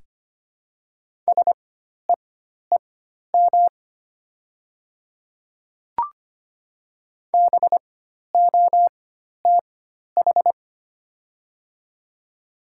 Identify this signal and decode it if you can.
Morse code, heard as a keyed tone, about 25 wpm, spaced as for 12 wpm, 700 Hz. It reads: SEEM BOTH
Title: 4x Character Spacing - Top 300 Words - 25wpm